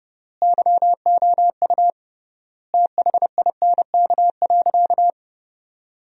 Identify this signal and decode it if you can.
Morse code, heard as a keyed tone, about 29 wpm, 700 Hz. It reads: YOU THINK.
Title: 3x Word-Spacing - Sentences from Top 400 Words - 30wpm